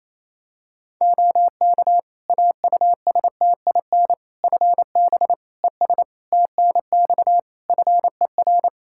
Morse code, heard as a keyed tone, about 28 wpm, 700 Hz. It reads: OK AUSTIN FB ES TNX FER
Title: Practice QSO - Exchange 2 - Info - 28wpm